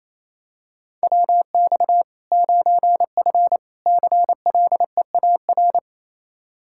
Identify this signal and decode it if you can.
Morse code, heard as a keyed tone, about 28 wpm, 700 Hz. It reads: WX 9F CLEAR
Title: K1USN SST QSO Practice 28wpm